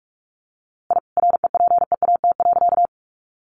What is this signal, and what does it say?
Morse code, heard as a keyed tone, about 44 wpm, 700 Hz. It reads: I REPEAT.
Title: A Sound Of Thunder 45wpm